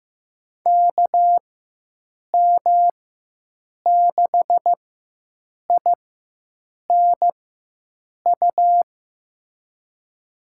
Morse code, heard as a keyed tone, about 15 wpm, spaced as for 7 wpm, 700 Hz. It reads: KM6INU
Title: USA Call Signs - 6 Characters - 4x Character Spacing - 15wpm